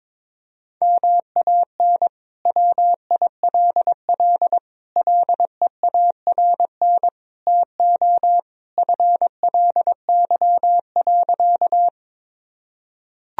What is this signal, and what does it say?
Morse code, heard as a keyed tone, about 21 wpm, 700 Hz. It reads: MAN WILL LEARN TO FLY.
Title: Sentences from Top 1000 Words - Review Part 2 - 22wpm